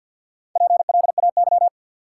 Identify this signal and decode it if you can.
Morse code, heard as a keyed tone, about 53 wpm, 700 Hz. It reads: PLAY
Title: Sentences from Top 200 Words - New Word Drill - 50wpm